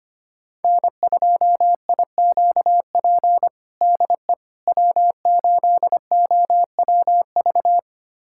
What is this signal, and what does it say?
Morse code, heard as a keyed tone, about 25 wpm, 700 Hz. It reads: N2IQP DE W8OWV
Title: USA Callsign DE Callsign - 5 Characters - 25wpm